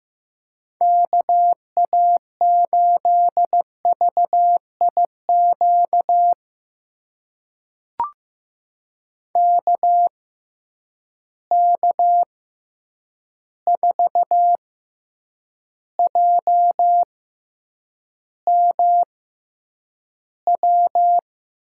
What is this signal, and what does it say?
Morse code, heard as a keyed tone, about 15 wpm, spaced as for 8 wpm, 700 Hz. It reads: KA8VIQ KK4JMW
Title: USA Call Signs - 6 Characters - 6x Character Spacing - 15wpm